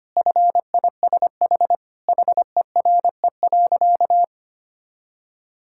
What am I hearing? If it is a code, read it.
Morse code, heard as a keyed tone, about 25 wpm, 700 Hz. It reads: FISH HERE.
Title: Sentences from Top 1000 Words - Review Part 6 - 25wpm